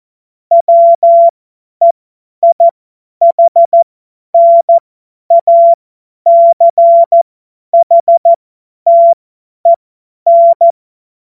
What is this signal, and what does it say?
Morse code, heard as a keyed tone, about 14 wpm, spaced as for 10 wpm, 680 Hz. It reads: WEIHNACHTEN
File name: Morsekurs / Witze / Witz 18. April 2023 50 F70